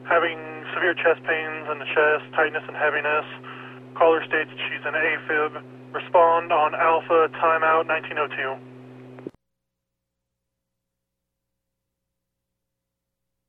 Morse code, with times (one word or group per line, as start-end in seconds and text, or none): none